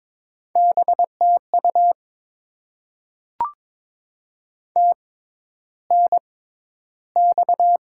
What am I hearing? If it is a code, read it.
Morse code, heard as a keyed tone, about 22 wpm, spaced as for 11 wpm, 700 Hz. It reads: BTU TNX
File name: Top 20 QSO Elements - 6x Character Spacing - 22wpm